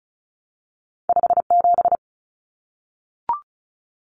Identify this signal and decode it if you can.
Morse code, heard as a keyed tone, about 36 wpm, 700 Hz. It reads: 57
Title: Numbers - 2-Digit 35wpm